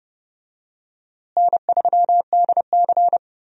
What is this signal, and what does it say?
Morse code, heard as a keyed tone, about 30 wpm, 700 Hz. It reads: N3DC
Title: USA Call Signs - 4 Characters - Repeated 2x - 30wpm